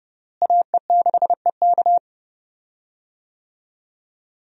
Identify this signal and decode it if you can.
Morse code, heard as a keyed tone, about 30 wpm, 700 Hz. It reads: AE6EK
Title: USA Call Signs - 5 Characters - Repeated 3x - 30wpm